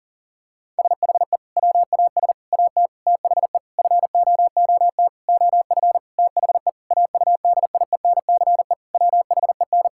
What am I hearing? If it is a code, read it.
Morse code, heard as a keyed tone, about 40 wpm, 700 Hz. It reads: SHE WAS AT THE FOOT OF THE AUDIENCE WHEN
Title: Sentences from Top 1000 Words - Review Part 5 - 40wpm